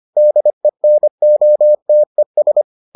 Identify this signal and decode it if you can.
Morse code, heard as a keyed tone, about 25 wpm, 600 Hz. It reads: DENOTES